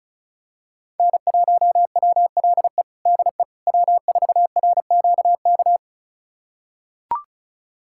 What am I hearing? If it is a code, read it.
Morse code, heard as a keyed tone, about 34 wpm, 700 Hz. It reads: N1WLE DE W4RQK
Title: USA Callsign DE Callsign - 5 Characters - 35wpm